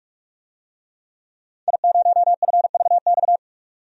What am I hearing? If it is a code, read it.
Morse code, heard as a keyed tone, about 44 wpm, 700 Hz. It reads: I0FVX
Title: DXCC-100 Call Signs - 5 Characters - 45wpm